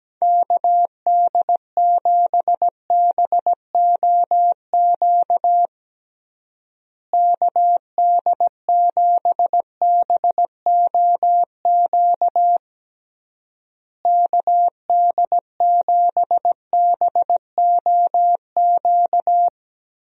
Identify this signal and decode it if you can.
Morse code, heard as a keyed tone, about 17 wpm, 700 Hz. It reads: KD7BOQ KD7BOQ KD7BOQ
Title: USA Call Signs - 6 Characters - Repeated 3x - 17wpm